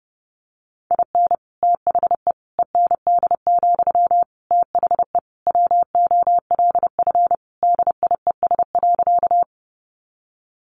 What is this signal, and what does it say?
Morse code, heard as a keyed tone, about 30 wpm, 700 Hz. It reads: IN THE END, THE WOLF DIES.
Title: Additional Easy Phrases 30wpm